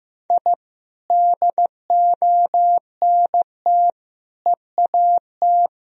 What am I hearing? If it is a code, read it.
Morse code, heard as a keyed tone, about 15 wpm, 700 Hz. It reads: I DONT EAT
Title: Sentences from Top 1000 Words - Review Part 3 - 15wpm